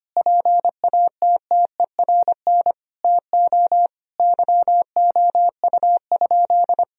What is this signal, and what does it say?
Morse code, heard as a keyed tone, about 25 wpm, 700 Hz. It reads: PATTERN TO YOU?
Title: Sentences from Top 1000 Words - Review Part 2 - 25wpm